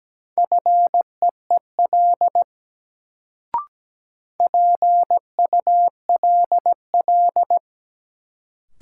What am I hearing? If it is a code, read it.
Morse code, heard as a keyed tone, about 17 wpm, 700 Hz. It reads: FEEL PULL
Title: Sentences from Top 400 Words - New Word Drill - 17wpm